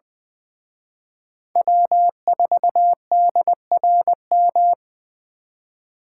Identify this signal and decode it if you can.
Morse code, heard as a keyed tone, about 24 wpm, 700 Hz. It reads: W4DRM
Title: Speed Racing - USA 1x3 Call Signs 20wpm